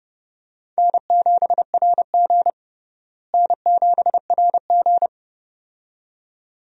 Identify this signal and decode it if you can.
Morse code, heard as a keyed tone, about 32 wpm, 700 Hz. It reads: N7RG N7RG
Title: USA Call Signs - 4 Characters - Repeated 2x - 30wpm